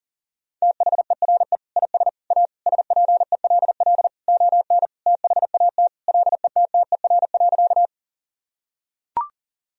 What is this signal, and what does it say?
Morse code, heard as a keyed tone, about 40 wpm, 700 Hz. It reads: THERE IS A SPELL ON THAT LETTER.
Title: Sentences from Top 1000 Words - Review Part 1 - 40wpm